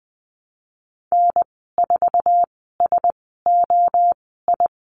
Morse code, heard as a keyed tone, about 20 wpm, spaced as for 15 wpm, 700 Hz. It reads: N4SOI
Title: 048 Lesson 9g - Number 4 - Call Signs 20wpm